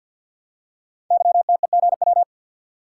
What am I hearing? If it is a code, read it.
Morse code, heard as a keyed tone, about 52 wpm, 700 Hz. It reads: YTEGW
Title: Code Groups - 5 Letter:Numbers - 50wpm